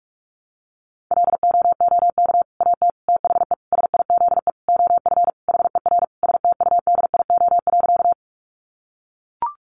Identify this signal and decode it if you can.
Morse code, heard as a keyed tone, about 44 wpm, 700 Hz. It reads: LOOK AT THE SIZE OF HER STUDIO.
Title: Sentences from Top 1200 Words 45wpm